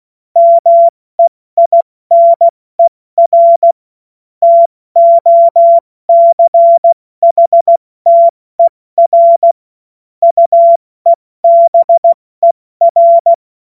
Morse code, heard as a keyed tone, about 16 wpm, 680 Hz. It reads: MEINER TOCHTER UEBER
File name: Morsekurs / Witze / Witz 16 April 2024 70 F80